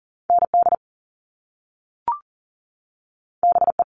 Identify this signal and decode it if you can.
Morse code, heard as a keyed tone, about 38 wpm, 700 Hz. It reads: ND BE
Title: All Cut Numbers - 2-Digit 40wpm